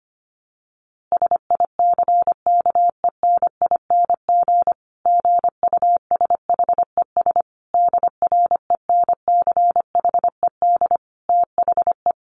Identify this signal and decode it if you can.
Morse code, heard as a keyed tone, about 25 wpm, 700 Hz. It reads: SICKENING GUSHES DRENCHED THE
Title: A Sound Of Thunder 25wpm